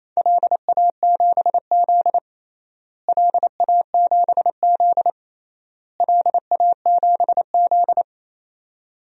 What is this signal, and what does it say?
Morse code, heard as a keyed tone, about 28 wpm, 700 Hz. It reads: LA7Z LA7Z LA7Z
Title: DXCC-100 Call Signs - 4 Characters - Repeated 3x - 28wpm